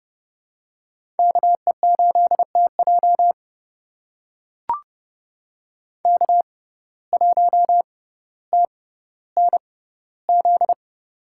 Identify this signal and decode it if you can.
Morse code, heard as a keyed tone, about 30 wpm, spaced as for 15 wpm, 700 Hz. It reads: KE8TJ K1TNZ
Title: USA Call Signs - 5 Characters - 6x Character Spacing - 30wpm